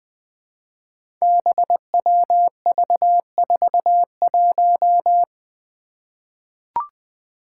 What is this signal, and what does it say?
Morse code, heard as a keyed tone, about 20 wpm, 700 Hz. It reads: BWV41
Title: Five Challenging Letter-Number Combinations 20wpm